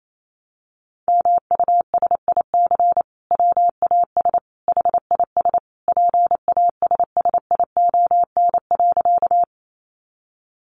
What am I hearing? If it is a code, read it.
Morse code, heard as a keyed tone, about 28 wpm, 700 Hz. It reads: MUSIC WAS HIS PASSION.
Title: Sentences from Top 1300 Words 28wpm